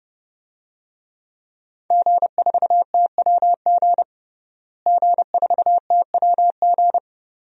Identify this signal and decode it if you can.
Morse code, heard as a keyed tone, about 30 wpm, 700 Hz. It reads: G4TWG G4TWG
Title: DXCC-100 Call Signs - 5 Characters - Repeated 2x - 30wpm